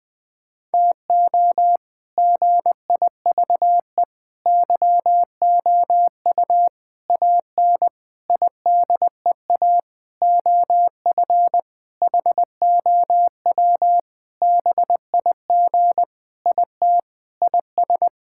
Morse code, heard as a keyed tone, about 20 wpm, 700 Hz. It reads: TO GIVE YOU AN IDEA OF HOW BIG IT IS
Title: Sentences with 7-Word Phrases - Spoken Beginning - 20wpm